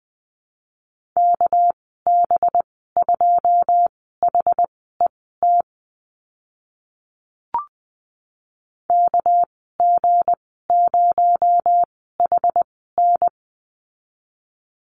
Morse code, heard as a keyed tone, about 20 wpm, spaced as for 14 wpm, 700 Hz. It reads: KB2HET KG0HN
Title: 231 Lesson 37g - Letter x - Call Signs 20wpm